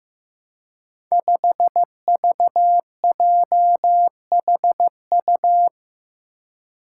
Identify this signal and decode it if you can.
Morse code, heard as a keyed tone, about 15 wpm, 700 Hz. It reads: 5VJHU
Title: Five Challenging Letter-Number Combinations 15wpm